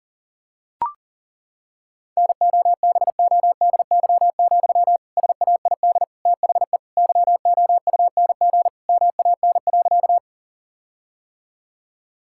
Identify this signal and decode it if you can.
Morse code, heard as a keyed tone, about 37 wpm, 700 Hz. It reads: NOBODY, SAID THE YOUNG MAN.
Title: Sentences from Top 1600 Words 40wpm